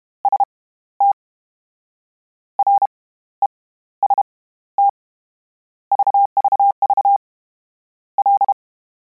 Morse code, heard as a keyed tone, about 32 wpm, spaced as for 16 wpm, 800 Hz. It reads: ST REST VVV L
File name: Combinations / Combinations-mp3s / Combinations@30